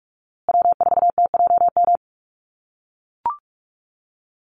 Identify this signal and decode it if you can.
Morse code, heard as a keyed tone, about 44 wpm, 700 Hz. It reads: W4TJM
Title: USA Call Signs Repeated 2x 45wpm